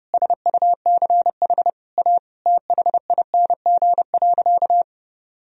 Morse code, heard as a keyed tone, about 30 wpm, 700 Hz. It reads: SUCH A THING.
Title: Sentences from Top 200 Words and Abbreviations 30wpm